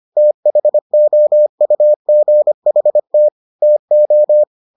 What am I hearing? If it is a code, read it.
Morse code, heard as a keyed tone, about 25 wpm, 600 Hz. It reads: THOUGHT TO